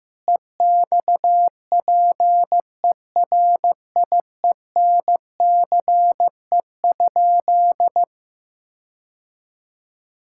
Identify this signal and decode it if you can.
Morse code, heard as a keyed tone, about 15 wpm, 700 Hz. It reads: EXPERIENCE?
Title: Sentences from Top 1600 Words 15wpm